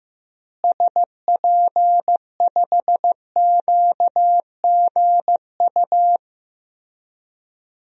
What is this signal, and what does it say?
Morse code, heard as a keyed tone, about 16 wpm, 700 Hz. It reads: SP5QGU
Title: DXCC-100 Call Signs - 6 Characters - Repeated 3x - 15wpm